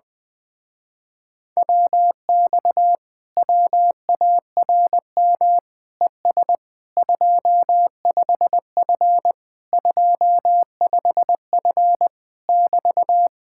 Morse code, heard as a keyed tone, about 20 wpm, 700 Hz. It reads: WX WARM ES 25F 25F <BT>
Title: Speed-Racing Common QSO Phrases - 17wpm